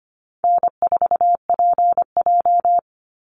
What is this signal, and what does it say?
Morse code, heard as a keyed tone, about 25 wpm, 700 Hz. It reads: N4PJ
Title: Easy Call Signs Repeated 2x - 25wpm